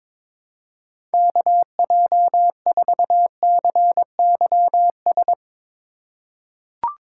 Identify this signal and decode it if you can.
Morse code, heard as a keyed tone, about 22 wpm, 700 Hz. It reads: KJ4CYS
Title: USA Call Signs - 3x Character Spacing - 22wpm